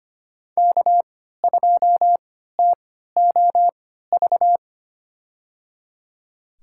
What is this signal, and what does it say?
Morse code, heard as a keyed tone, about 25 wpm, spaced as for 14 wpm, 700 Hz. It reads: K2TOV
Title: USA Call Signs - 5 Characters - 3x Character Spacing - 25wpm